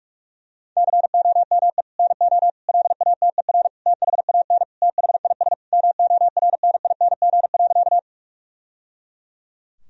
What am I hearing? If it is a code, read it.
Morse code, heard as a keyed tone, about 42 wpm, 700 Hz. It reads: COME NO LATER THAN THIS MORNING.
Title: Sentences from Top 1000 Words - Review Part 4 - 45wpm